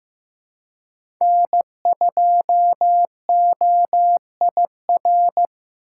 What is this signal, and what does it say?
Morse code, heard as a keyed tone, about 15 wpm, 700 Hz. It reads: N2OIR
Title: USA Call Signs - 5 Characters - 4x Character Spacing - 15wpm